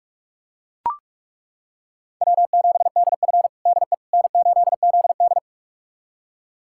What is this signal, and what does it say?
Morse code, heard as a keyed tone, about 41 wpm, 700 Hz. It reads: W7DF DE N8ZD
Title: USA Callsign DE Callsign - 4 Characters - 45wpm